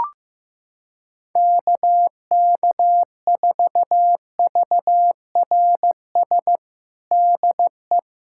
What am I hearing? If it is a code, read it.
Morse code, heard as a keyed tone, about 15 wpm, 700 Hz. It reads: KK4VRS DE
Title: USA Callsign DE Callsign - 6 Characters - 15wpm